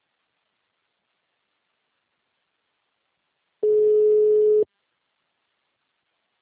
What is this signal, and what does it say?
Ringback tone (Europe, 425 Hz).